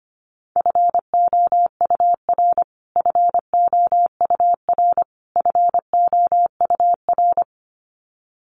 Morse code, heard as a keyed tone, about 25 wpm, 700 Hz. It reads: FOUR FOUR FOUR
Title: Top 300 Words Repeated 25wpm